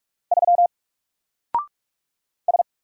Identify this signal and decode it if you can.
Morse code, heard as a keyed tone, about 44 wpm, 700 Hz. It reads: 3 S
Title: Single Character (Letter:Number:Punctuation:Prosign) 45wpm